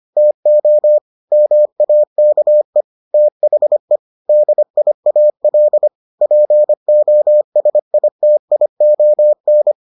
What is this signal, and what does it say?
Morse code, heard as a keyed tone, about 25 wpm, 600 Hz. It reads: TO MAKE THE DIAL POSITION